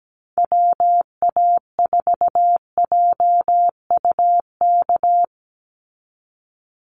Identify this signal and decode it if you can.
Morse code, heard as a keyed tone, about 17 wpm, 700 Hz. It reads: WA4JUK
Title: Call Signs Repeated 3x - 17wpm